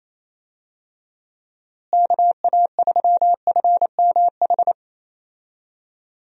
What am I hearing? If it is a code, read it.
Morse code, heard as a keyed tone, about 28 wpm, 700 Hz. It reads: KA3FMH
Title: USA Call Signs - 6 Characters - Repeated 2x - 28wpm